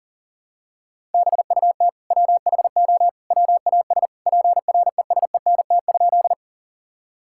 Morse code, heard as a keyed tone, about 40 wpm, 700 Hz. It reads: BUT WHO WAS PRESENT?
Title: Sentences from Top 1000 Words - Review Part 5 - 40wpm